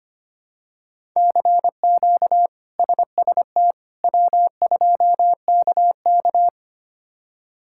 Morse code, heard as a keyed tone, about 25 wpm, 700 Hz. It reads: CQ SST W2KK
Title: K1USN SST QSO Practice - 25wpm